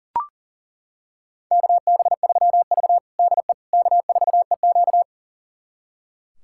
Morse code, heard as a keyed tone, about 40 wpm, 700 Hz. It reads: KB3V DE K4EQ
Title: USA Callsign DE Callsign - 4 Characters - 40wpm